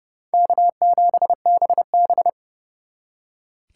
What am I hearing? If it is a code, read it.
Morse code, heard as a keyed tone, about 36 wpm, 700 Hz. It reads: K7BB
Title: Speed Racing - USA 1x2 Call Signs 30wpm